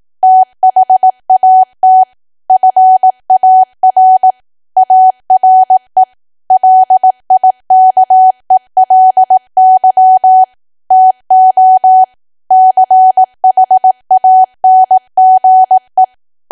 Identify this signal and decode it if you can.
Morse code, heard as a keyed tone, about 18 wpm, 750 Hz. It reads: THAT FAR ARE LIKELY TO CHANGE
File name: cwa / W1AW-Level2 / 18WPM